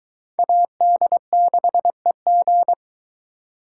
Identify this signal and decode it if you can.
Morse code, heard as a keyed tone, about 20 wpm, 700 Hz. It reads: AD6EG